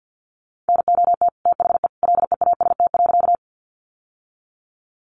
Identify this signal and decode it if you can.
Morse code, heard as a keyed tone, about 48 wpm, 700 Hz. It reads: NOT THE LEAST.
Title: Sentences from Top 700 Words 50wpm